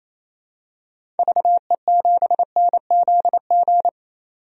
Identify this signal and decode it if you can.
Morse code, heard as a keyed tone, about 28 wpm, 700 Hz. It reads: VE7NZG
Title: DXCC-100 Call Signs - 6 Characters - 28wpm